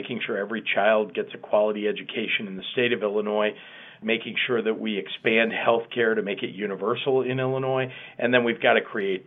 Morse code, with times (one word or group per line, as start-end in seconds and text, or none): none